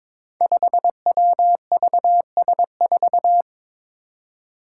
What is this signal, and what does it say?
Morse code, heard as a keyed tone, about 22 wpm, 700 Hz. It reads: 5WVS4
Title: Five Challenging Letter-Number Combinations 22wpm